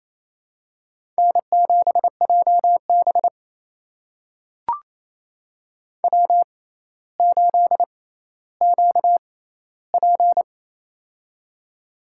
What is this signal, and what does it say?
Morse code, heard as a keyed tone, about 28 wpm, spaced as for 14 wpm, 700 Hz. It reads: N7JB W8QP
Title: USA Call Signs - 4 Characters - 6x Character Spacing - 28wpm